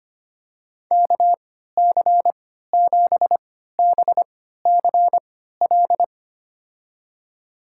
Morse code, heard as a keyed tone, about 25 wpm, spaced as for 14 wpm, 700 Hz. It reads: KC7BCL
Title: USA Call Signs - 6 Characters - 3x Character Spacing - 25wpm